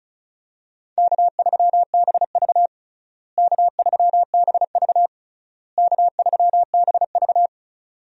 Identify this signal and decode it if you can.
Morse code, heard as a keyed tone, about 36 wpm, 700 Hz. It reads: K3BV K3BV K3BV
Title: USA Call Signs - 4 Characters - Repeated 3x - 35wpm